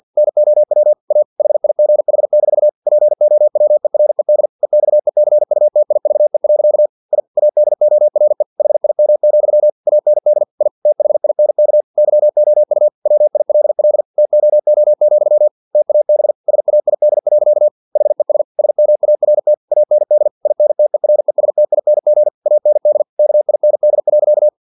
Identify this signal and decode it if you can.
Morse code, heard as a keyed tone, about 50 wpm, 600 Hz. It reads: NOW A HIGH- POWERED EXECUTIVE. I ADORE HIM, AND I THINK YOU WILL TOO, TAB SAID. HES SMART AND INTERESTING AND KIND.
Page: Never 49wpm